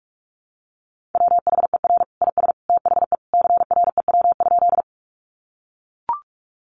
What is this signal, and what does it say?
Morse code, heard as a keyed tone, about 44 wpm, 700 Hz. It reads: WHER IS THE CREW?
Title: Sentences from Top 1500 Words 45wpm